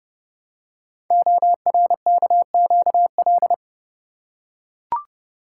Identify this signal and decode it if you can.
Morse code, heard as a keyed tone, about 30 wpm, 700 Hz. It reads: ORKQL
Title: Code Groups - 5 Letters - 30wpm